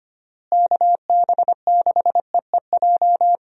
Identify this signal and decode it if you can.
Morse code, heard as a keyed tone, about 25 wpm, 700 Hz. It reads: KB6EEJ
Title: USA Call Signs - 6 Characters - Repeated 2x - 25wpm